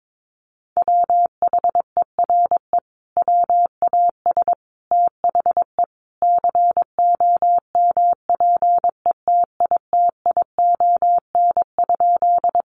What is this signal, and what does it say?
Morse code, heard as a keyed tone, about 22 wpm, 700 Hz. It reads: WHERE WAS THE COMPETITION?
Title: Sentences from Top 1000 Words 22wpm